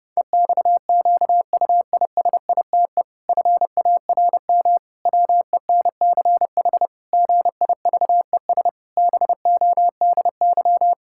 Morse code, heard as a keyed tone, about 30 wpm, 700 Hz. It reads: EXQUISITE FARM WENCH GIVES BODY
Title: Pangram Challenge v2 - 30wpm